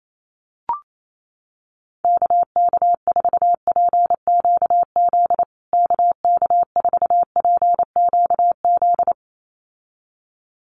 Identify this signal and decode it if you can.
Morse code, heard as a keyed tone, about 28 wpm, 700 Hz. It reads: KK4PQZ KK4PQZ
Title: USA Call Signs Repeated 2x 28wpm